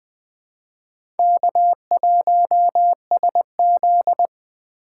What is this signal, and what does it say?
Morse code, heard as a keyed tone, about 20 wpm, 700 Hz. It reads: K1SZ
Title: USA Call Signs - 4 Characters - 4x Character Spacing - 20wpm